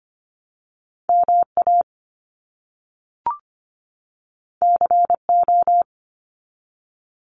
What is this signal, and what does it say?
Morse code, heard as a keyed tone, about 25 wpm, 700 Hz. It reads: MA CO
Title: US States 25wpm